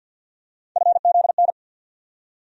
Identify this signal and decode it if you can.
Morse code, heard as a keyed tone, about 48 wpm, 700 Hz. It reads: FZN
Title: USA Call Sign Postfixes - 3 Letters 50wpm